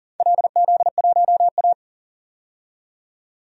Morse code, heard as a keyed tone, about 42 wpm, 700 Hz. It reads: LZ1A
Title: DXCC-100 Call Signs - 4 Characters - 40wpm